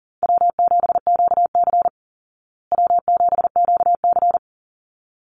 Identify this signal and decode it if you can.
Morse code, heard as a keyed tone, about 40 wpm, 700 Hz. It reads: W7QC W7QC
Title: Easy Call Signs Repeated 2x - 40wpm